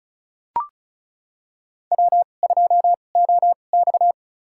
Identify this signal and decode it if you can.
Morse code, heard as a keyed tone, about 36 wpm, spaced as for 25 wpm, 700 Hz. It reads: W2OX
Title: USA Call Signs - 4 Characters - 2x Character Spacing - 35wpm